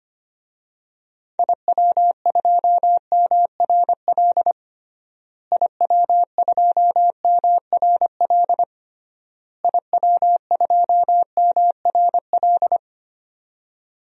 Morse code, heard as a keyed tone, about 25 wpm, 700 Hz. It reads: IW2MRL IW2MRL IW2MRL
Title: DXCC-100 Call Signs - 6 Characters - Repeated 3x - 25wpm